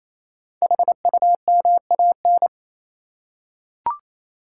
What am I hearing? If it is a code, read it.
Morse code, heard as a keyed tone, about 27 wpm, 690 Hz. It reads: HUMAN
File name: Easy Words - Speed-Racing with Hints - 22wpm